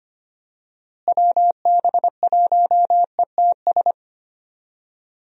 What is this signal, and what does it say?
Morse code, heard as a keyed tone, about 25 wpm, 700 Hz. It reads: WB1ETS
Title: USA Call Signs - 6 Characters - Repeated 2x - 25wpm